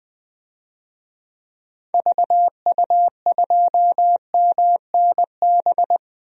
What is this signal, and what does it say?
Morse code, heard as a keyed tone, about 20 wpm, 700 Hz. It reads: VU2MNB
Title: DXCC-100 Call Signs - 6 Characters - 20wpm